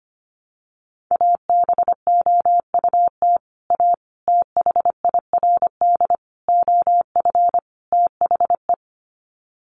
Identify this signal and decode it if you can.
Morse code, heard as a keyed tone, about 25 wpm, 700 Hz. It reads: ABOUT A THIRD OF THE
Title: Sets of 5 Words Encore2 25wpm